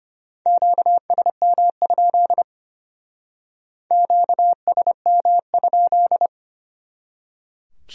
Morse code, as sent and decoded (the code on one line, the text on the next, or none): --.- ... -- ..--.. / --.- ... -- ..--..
QSM? QSM?